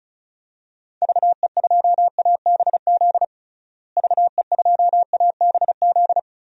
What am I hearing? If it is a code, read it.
Morse code, heard as a keyed tone, about 36 wpm, 700 Hz. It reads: VE2ABZ VE2ABZ
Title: DXCC-100 Call Signs - 6 Characters - Repeated 2x - 35wpm